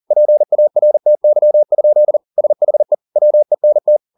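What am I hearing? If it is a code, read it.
Morse code, heard as a keyed tone, about 40 wpm, 600 Hz. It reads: PARTY? SHE WENT
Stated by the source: Morse code at 40 wpm, practice text, CW med Gnister.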